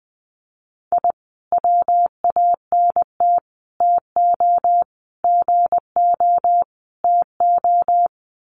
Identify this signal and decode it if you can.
Morse code, heard as a keyed tone, about 20 wpm, 700 Hz. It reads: I WANT TO GO TO
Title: Sets of 5 Words Encore 20wpm